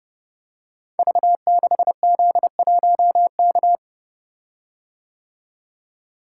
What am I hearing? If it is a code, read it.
Morse code, heard as a keyed tone, about 30 wpm, 700 Hz. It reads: V6Z1K
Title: Code Groups - 5 Letter:Numbers - 30wpm